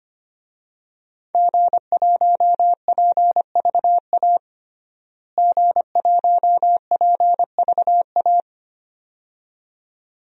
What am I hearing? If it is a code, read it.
Morse code, heard as a keyed tone, about 25 wpm, 700 Hz. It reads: G1PVA G1PVA
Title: DXCC-100 Call Signs - 5 Characters - Repeated 2x - 25wpm